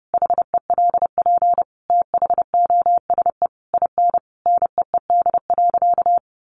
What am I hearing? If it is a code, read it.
Morse code, heard as a keyed tone, about 30 wpm, 700 Hz. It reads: HELP THOSE IN NEED.
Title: Sentences from Top 1200 Words 30wpm